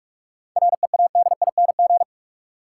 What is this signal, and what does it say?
Morse code, heard as a keyed tone, about 46 wpm, 700 Hz. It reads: READING
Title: Sentences from Top 600 Words - New Word Drill - 45wpm